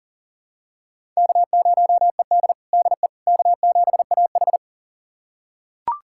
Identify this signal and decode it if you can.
Morse code, heard as a keyed tone, about 40 wpm, 700 Hz. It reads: K0ED DE K7AH
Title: USA Callsign DE Callsign - 4 Characters - 40wpm